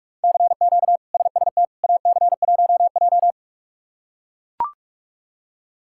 Morse code, heard as a keyed tone, about 42 wpm, 700 Hz. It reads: CQ SST AC1J
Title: K1USN SST QSO Practice - 45wpm